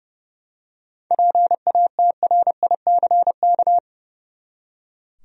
Morse code, heard as a keyed tone, about 30 wpm, 700 Hz. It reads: PATRICK
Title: USA Common Names - 30wpm